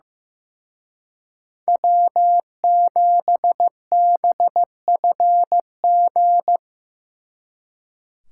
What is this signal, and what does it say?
Morse code, heard as a keyed tone, about 16 wpm, 700 Hz. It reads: W7BFG
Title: Speed Racing - USA 1x3 Call Signs 15wpm